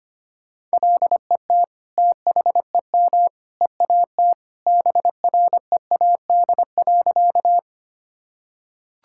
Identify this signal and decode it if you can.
Morse code, heard as a keyed tone, about 25 wpm, 700 Hz. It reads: LET THEM EAT BREAD.
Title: Sentences from Top 1600 Words 25wpm